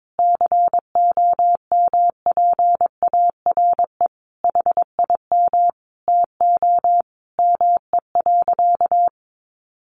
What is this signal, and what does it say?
Morse code, heard as a keyed tone, about 22 wpm, 700 Hz. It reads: COMPARE HIM TO ME.
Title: Sentences from Top 1300 Words 22wpm